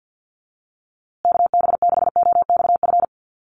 Morse code, heard as a keyed tone, about 50 wpm, 700 Hz. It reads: KB6OXF
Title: USA Call Signs Repeated 2x 50wpm